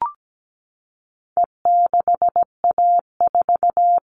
Morse code, heard as a keyed tone, about 17 wpm, 700 Hz. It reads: E6A4
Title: All Cut Numbers - 4-Digit 17wpm